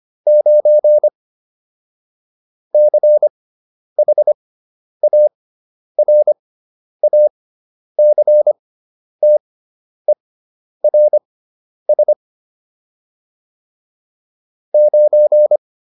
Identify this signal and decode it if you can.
Morse code, heard as a keyed tone, about 25 wpm, spaced as for 10 wpm, 600 Hz. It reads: 9 CHARACTERS 9